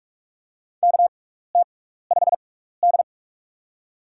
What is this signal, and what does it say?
Morse code, heard as a keyed tone, about 44 wpm, spaced as for 15 wpm, 690 Hz. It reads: KT5D
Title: USA Call Signs - 6x Character Spacing - 45wpm